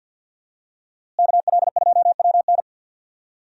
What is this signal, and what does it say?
Morse code, heard as a keyed tone, about 49 wpm, 700 Hz. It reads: KL2WN
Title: USA Call Signs - 5 Characters - Repeated 3x - 50wpm